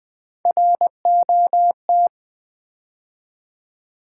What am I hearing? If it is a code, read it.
Morse code, heard as a keyed tone, about 20 wpm, 700 Hz. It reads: ROT
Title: Build A Word - 3 Letters At A Time - 8-Letter Words - 20wpm